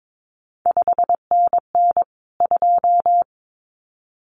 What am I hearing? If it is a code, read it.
Morse code, heard as a keyed tone, about 22 wpm, 700 Hz. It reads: 5NN 2
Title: CQ WW DX Contest Exchange - 22wpm